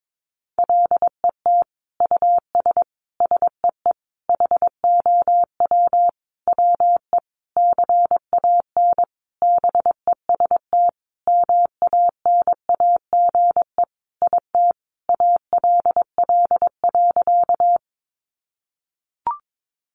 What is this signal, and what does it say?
Morse code, heard as a keyed tone, about 22 wpm, 700 Hz. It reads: LET US SEE HOW WE CAN BEST MANAGE IT ALL.
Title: Sentences from Top 1400 Words 22wpm